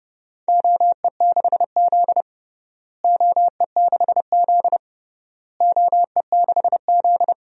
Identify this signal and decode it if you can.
Morse code, heard as a keyed tone, about 31 wpm, 700 Hz. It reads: OE6Z OE6Z OE6Z
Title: DXCC-100 Call Signs - 4 Characters - Repeated 3x - 30wpm